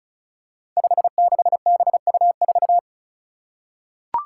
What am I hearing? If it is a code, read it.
Morse code, heard as a keyed tone, about 36 wpm, 700 Hz. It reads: 56BU4
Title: Five Challenging Letter-Number Combinations 35wpm